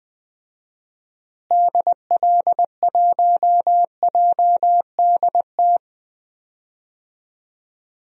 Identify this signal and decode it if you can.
Morse code, heard as a keyed tone, about 20 wpm, 700 Hz. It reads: DL1JDT
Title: DXCC-100 Call Signs - 6 Characters - Repeated 3x - 20wpm